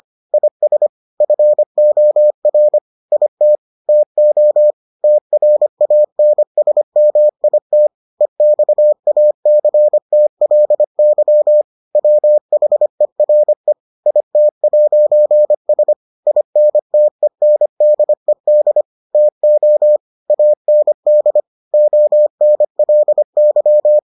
Morse code, heard as a keyed tone, about 25 wpm, 600 Hz. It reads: IS FOR IT TO TRANSMIT EXACTLY WHERE IT'S INTENDED TO AND ONLY